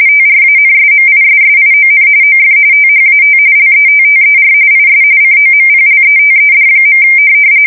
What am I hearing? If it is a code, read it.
RTTY, about 45.45 baud. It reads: TRANSMISSION OF THAT LETTER IS COMPLETED.
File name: fsk170 / morkrum